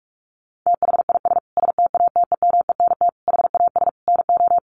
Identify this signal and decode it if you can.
Morse code, heard as a keyed tone, about 44 wpm, 700 Hz. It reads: THIS STATEMENT HAS NO
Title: Sentences from Top 1200 Words 45wpm